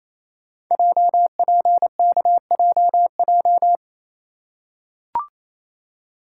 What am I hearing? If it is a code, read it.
Morse code, heard as a keyed tone, about 27 wpm, 700 Hz. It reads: JPKJJ
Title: Code Groups - 5 Letter:Numbers - 28wpm